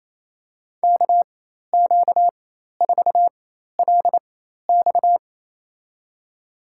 Morse code, heard as a keyed tone, about 28 wpm, spaced as for 13 wpm, 700 Hz. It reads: KQ4LX
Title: USA Call Signs - 4x Character Spacing - 28wpm